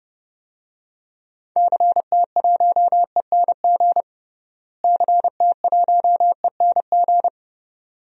Morse code, heard as a keyed tone, about 30 wpm, 700 Hz. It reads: CT1ENG CT1ENG
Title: DXCC-100 Call Signs - 6 Characters - Repeated 2x - 30wpm